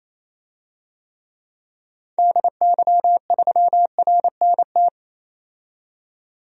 Morse code, heard as a keyed tone, about 28 wpm, 700 Hz. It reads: DY3RNT
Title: DXCC-100 Call Signs - 6 Characters - 28wpm